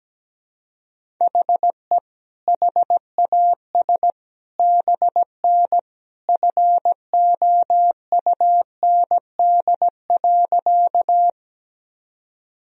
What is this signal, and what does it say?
Morse code, heard as a keyed tone, about 17 wpm, 700 Hz. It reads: HE HAS BN FOUND.
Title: Sentences from Top 300 Words and Abbreviations 17wpm